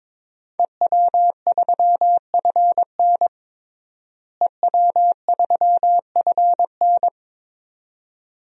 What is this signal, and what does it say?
Morse code, heard as a keyed tone, about 22 wpm, 700 Hz. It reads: EW3FN EW3FN
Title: DXCC-100 Call Signs - 5 Characters - Repeated 2x - 22wpm